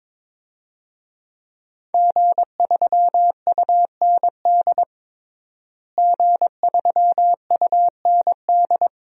Morse code, heard as a keyed tone, about 22 wpm, 700 Hz. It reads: G3UND G3UND
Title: DXCC-100 Call Signs - 5 Characters - Repeated 3x - 22wpm